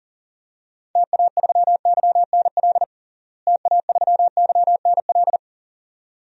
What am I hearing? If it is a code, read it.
Morse code, heard as a keyed tone, about 37 wpm, 700 Hz. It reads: TA3YNL TA3YNL
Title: DXCC-100 Call Signs - 6 Characters - Repeated 2x - 40wpm